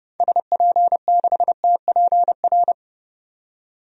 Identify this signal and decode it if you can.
Morse code, heard as a keyed tone, about 30 wpm, 700 Hz. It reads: SP6TPR
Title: DXCC-100 Call Signs - 6 Characters - 30wpm